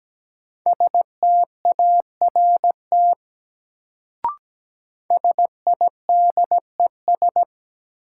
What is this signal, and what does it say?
Morse code, heard as a keyed tone, about 17 wpm, 700 Hz. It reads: START SIDES
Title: Sets of 4 Words - New Word Drill - 17wpm